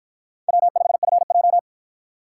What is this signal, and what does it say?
Morse code, heard as a keyed tone, about 48 wpm, 700 Hz. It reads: W5FJ